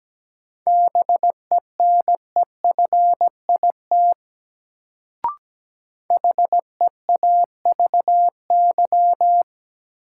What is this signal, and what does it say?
Morse code, heard as a keyed tone, about 17 wpm, 700 Hz. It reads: BENEFIT HEAVY
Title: Sentences from Top 1000 Words - Review Part 6 - New Word Drill - 17wpm